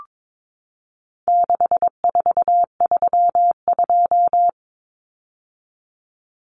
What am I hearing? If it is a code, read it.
Morse code, heard as a keyed tone, about 22 wpm, 700 Hz. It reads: 6432
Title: Common Cut Numbers - 4-Digit 22wpm